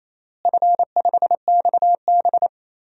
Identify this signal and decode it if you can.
Morse code, heard as a keyed tone, about 28 wpm, 700 Hz. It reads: F5XB
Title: DXCC-100 Call Signs - 4 Characters - Repeated 3x - 28wpm